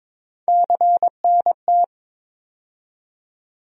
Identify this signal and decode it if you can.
Morse code, heard as a keyed tone, about 22 wpm, 700 Hz. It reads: CNT
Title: USA Call Sign Postfixes - 3 Letters 22wpm